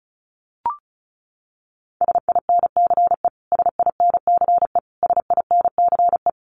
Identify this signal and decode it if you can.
Morse code, heard as a keyed tone, about 34 wpm, 700 Hz. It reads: SINCE SINCE SINCE
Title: Top 300 Words Repeated 35wpm